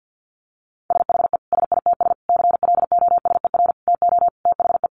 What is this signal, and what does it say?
Morse code, heard as a keyed tone, about 50 wpm, 700 Hz. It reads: SHE SITS CLOSER TO THE
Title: Additional Easy Phrases 50wpm